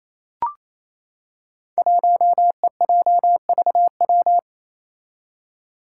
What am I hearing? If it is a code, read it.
Morse code, heard as a keyed tone, about 27 wpm, 700 Hz. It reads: 1EJVW
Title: Five Challenging Letter-Number Combinations 28wpm